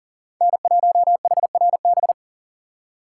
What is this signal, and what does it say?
Morse code, heard as a keyed tone, about 52 wpm, 700 Hz. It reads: N1HRB